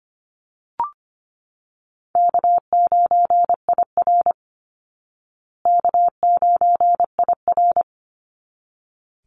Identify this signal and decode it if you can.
Morse code, heard as a keyed tone, about 25 wpm, 700 Hz. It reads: K9IR K9IR
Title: Easy Call Signs Repeated 2x - 25wpm